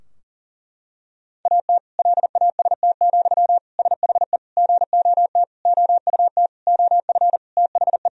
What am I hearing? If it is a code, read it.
Morse code, heard as a keyed tone, about 40 wpm, 700 Hz. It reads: AT LAST, SHE GOT OUT OF THE
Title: Sentences from Top 1000 Words - Review Part 2 - 40wpm